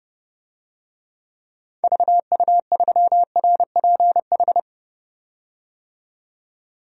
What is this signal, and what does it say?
Morse code, heard as a keyed tone, about 29 wpm, 700 Hz. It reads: VU3RPH
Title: DXCC-100 Call Signs - 6 Characters - 30wpm